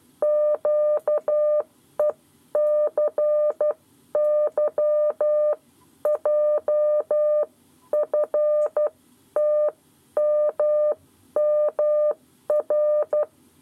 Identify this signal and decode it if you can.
Morse code, heard as a keyed tone, about 12 wpm, spaced as for 10 wpm, 600 Hz. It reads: QECYJFTMMR
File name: 7 Lezione cw 1